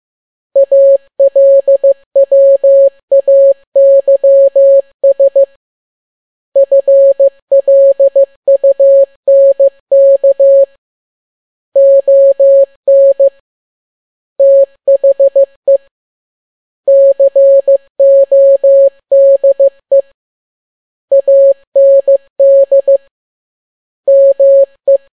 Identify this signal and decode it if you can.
Morse code, heard as a keyed tone, about 15 wpm, 550 Hz. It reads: ALWAYS FLUNK ON THE CODE AND ME
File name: n6tt p1 15C13